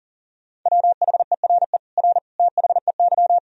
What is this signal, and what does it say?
Morse code, heard as a keyed tone, about 37 wpm, 700 Hz. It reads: WHERE R THEY
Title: Sentences from Top 200 Words and Abbreviations 40wpm